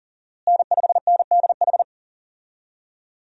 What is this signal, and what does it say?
Morse code, heard as a keyed tone, about 40 wpm, 700 Hz. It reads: N5NDH
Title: USA Call Signs - 5 Characters - Repeated 2x - 40wpm